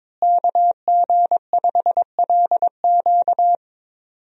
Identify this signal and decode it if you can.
Morse code, heard as a keyed tone, about 22 wpm, 700 Hz. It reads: KG5LQ
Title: USA Call Signs - 5 Characters - Repeated 3x - 22wpm